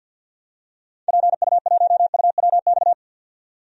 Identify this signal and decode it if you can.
Morse code, heard as a keyed tone, about 49 wpm, 700 Hz. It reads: PU1UWX
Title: DXCC-100 Call Signs - 6 Characters - 50wpm